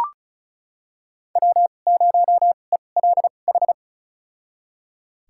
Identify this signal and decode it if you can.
Morse code, heard as a keyed tone, about 36 wpm, spaced as for 25 wpm, 700 Hz. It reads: W0ELH
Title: USA Call Signs - 2x Character Spacing - 35wpm